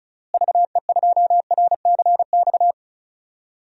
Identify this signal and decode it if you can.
Morse code, heard as a keyed tone, about 36 wpm, 700 Hz. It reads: VE2RCX
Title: DXCC-100 Call Signs - 6 Characters - Repeated 3x - 35wpm